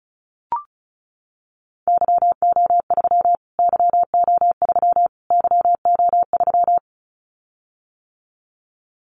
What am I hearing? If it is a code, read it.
Morse code, heard as a keyed tone, about 34 wpm, 700 Hz. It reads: YO3 YO3 YO3
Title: Callsign Prefixes for Top 25 DXCC - Repeated 3x - 35wpm